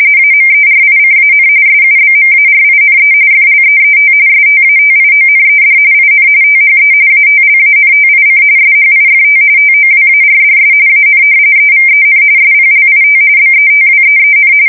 RTTY, about 45.45 baud. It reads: AMERICAN SUBMARINE FLEET STATIONED IN THE ATLANTIC SHIFT TOWARDS AN EVEN SPLIT BETWEEN
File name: fsk170 / usnavy5